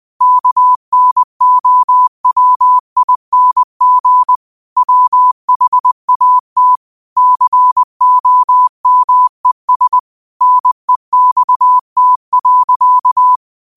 Morse code, heard as a keyed tone, about 20 wpm, 1000 Hz. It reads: KNOWING WHAT COMES NEXT.